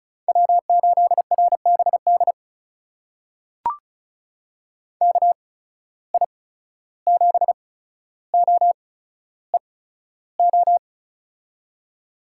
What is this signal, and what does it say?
Morse code, heard as a keyed tone, about 36 wpm, spaced as for 14 wpm, 700 Hz. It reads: W8RBD KI7OEO
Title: USA Call Signs - 8x Character Spacing - 35wpm